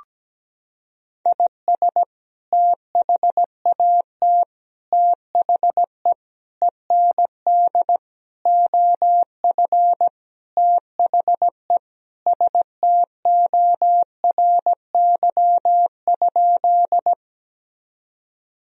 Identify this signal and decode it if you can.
Morse code, heard as a keyed tone, about 17 wpm, 700 Hz. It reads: IS THAT THE END OF THE STORY?
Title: Sentences from Top 1000 Words - Review Part 1 - 17wpm